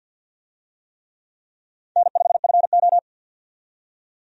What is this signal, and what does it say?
Morse code, heard as a keyed tone, about 51 wpm, 700 Hz. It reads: N5FO
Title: USA Call Signs - 4 Characters - 50wpm